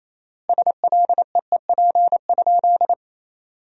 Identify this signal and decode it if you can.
Morse code, heard as a keyed tone, about 28 wpm, 700 Hz. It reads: SLEEP?
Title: 4x Word-Spacing - Sentences from Top 500 Words - 28wpm